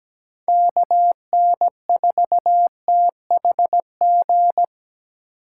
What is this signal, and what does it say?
Morse code, heard as a keyed tone, about 17 wpm, 700 Hz. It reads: KN4THG
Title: USA Call Signs - 6 Characters - 2x Character Spacing - 17wpm